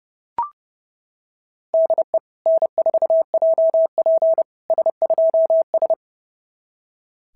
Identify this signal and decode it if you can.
Morse code, heard as a keyed tone, about 30 wpm, 690 Hz. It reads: DE N4JP S2S
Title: SOTA QSO Practice 30wpm